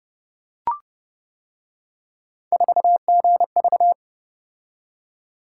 Morse code, heard as a keyed tone, about 29 wpm, 700 Hz. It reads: 4GV